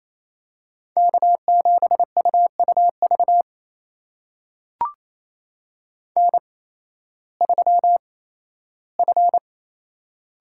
Morse code, heard as a keyed tone, about 28 wpm, spaced as for 15 wpm, 700 Hz. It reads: K7UUV N3F
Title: USA Call Signs - 5 Characters - 8x Character Spacing - 28wpm